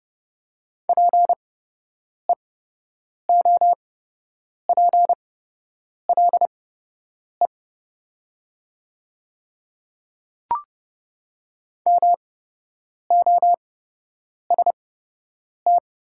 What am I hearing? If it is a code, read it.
Morse code, heard as a keyed tone, about 30 wpm, spaced as for 8 wpm, 700 Hz. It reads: PEOPLE MOST
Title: 8x Character Spacing - Top 100 Words - 30wpm